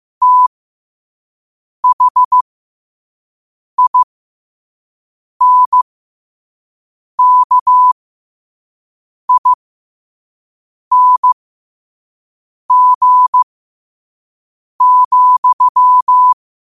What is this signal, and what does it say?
Morse code, heard as a keyed tone, about 15 wpm, spaced as for 5 wpm, 1000 Hz. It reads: THINKING,
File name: qotdfiles / 5wpm